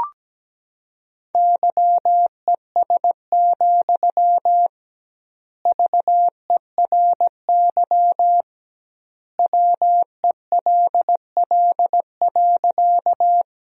Morse code, heard as a keyed tone, about 17 wpm, 700 Hz. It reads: YES, VERY WELL.
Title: 2x Word-Spacing - Sentences from Top 500 Words - 17wpm